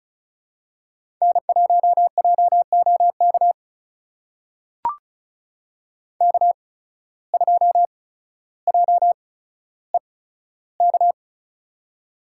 Morse code, heard as a keyed tone, about 36 wpm, spaced as for 15 wpm, 700 Hz. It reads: N1JOK K2JEK
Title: USA Call Signs - 5 Characters - 8x Character Spacing - 35wpm